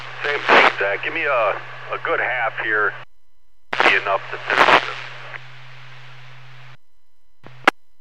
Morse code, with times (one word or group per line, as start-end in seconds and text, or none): none